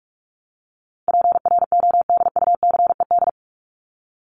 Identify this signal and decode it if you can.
Morse code, heard as a keyed tone, about 44 wpm, 700 Hz. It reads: PRODUCED
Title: ED Suffix - Top 100 Words with Base Words 45wpm